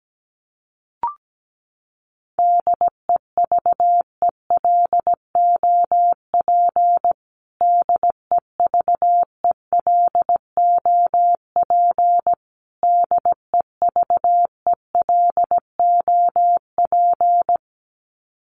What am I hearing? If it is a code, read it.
Morse code, heard as a keyed tone, about 17 wpm, 700 Hz. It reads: DEVELOP DEVELOP DEVELOP
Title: Top 500 Words Repeated 17wpm